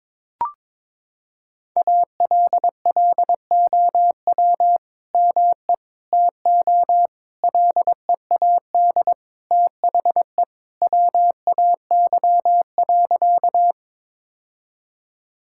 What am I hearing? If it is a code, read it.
Morse code, heard as a keyed tone, about 22 wpm, 700 Hz. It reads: ALLOW ME TO LEAD THE WAY.
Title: Sentences from Top 1000 Words - Review Part 5 - 22wpm